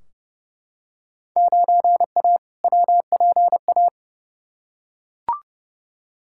Morse code, heard as a keyed tone, about 32 wpm, 700 Hz. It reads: 9A WPA
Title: Field Day Exchanges Encore - 30wpm